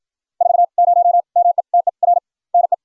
Morse code, heard as a keyed tone, about 56 wpm, spaced as for 46 wpm, 700 Hz. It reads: 40 METER NE